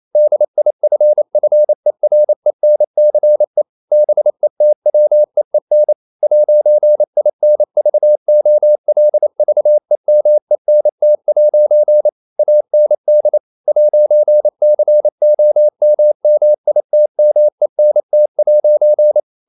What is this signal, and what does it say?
Morse code, heard as a keyed tone, about 28 wpm, 600 Hz. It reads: DIFFERENCE BETWEEN 'INVOLVEMENT' AND 'COMMITMENT'
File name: CW / Practice / EnglishQuotations0128